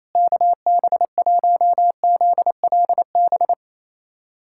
Morse code, heard as a keyed tone, about 28 wpm, 700 Hz. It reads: KB1ZLB
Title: USA Callsign DE Callsign 28wpm